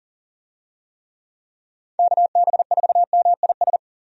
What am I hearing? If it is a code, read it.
Morse code, heard as a keyed tone, about 40 wpm, 700 Hz. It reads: KB4MIS
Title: USA Call Signs - 6 Characters - 40wpm